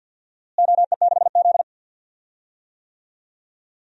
Morse code, heard as a keyed tone, about 51 wpm, 700 Hz. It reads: OE6Z
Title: DXCC-100 Call Signs - 4 Characters - Repeated 2x - 50wpm